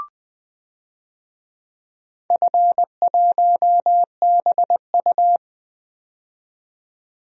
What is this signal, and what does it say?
Morse code, heard as a keyed tone, about 20 wpm, 700 Hz. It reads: F1BU
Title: DXCC-100 Call Signs - 4 Characters - 20wpm